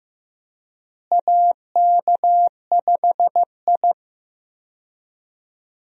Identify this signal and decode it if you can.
Morse code, heard as a keyed tone, about 16 wpm, 700 Hz. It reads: AK5I
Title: USA Call Signs - 4 Characters - Repeated 3x - 15wpm